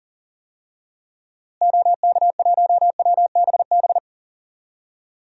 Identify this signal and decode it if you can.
Morse code, heard as a keyed tone, about 40 wpm, 700 Hz. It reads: OK1WBB
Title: DXCC-100 Call Signs - 6 Characters - Repeated 2x - 40wpm